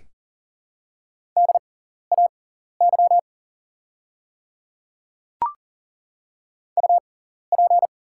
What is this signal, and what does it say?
Morse code, heard as a keyed tone, about 37 wpm, spaced as for 13 wpm, 690 Hz. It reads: DAY UP
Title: 6x Character Spacing - Top 100 Words - 40wpm